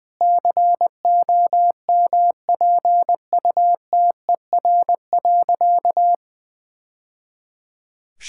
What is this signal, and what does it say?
Morse code, heard as a keyed tone, about 20 wpm, 700 Hz. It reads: COMPUTER.
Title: Easy Sentences with Hints - 20wpm